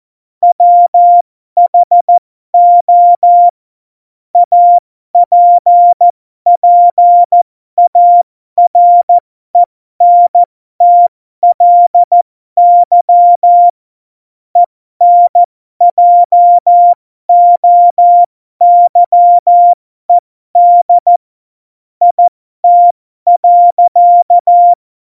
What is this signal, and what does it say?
Morse code, heard as a keyed tone, about 14 wpm, spaced as for 12 wpm, 700 Hz. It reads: WHO APPARENTLY ENJOYED IT.
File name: Morsekurs / Witze